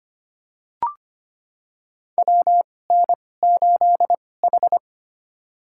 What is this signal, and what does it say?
Morse code, heard as a keyed tone, about 25 wpm, spaced as for 18 wpm, 700 Hz. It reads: WN8H
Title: USA Call Signs - 4 Characters - 2x Character Spacing - 25wpm